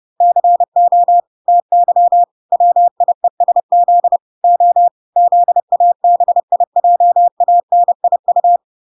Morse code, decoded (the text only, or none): CO TY WIESZ O ZABIJANIU